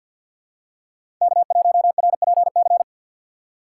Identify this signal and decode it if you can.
Morse code, heard as a keyed tone, about 49 wpm, 700 Hz. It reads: K1RPC
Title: USA Call Signs - 5 Characters - 4x Character Spacing - 50wpm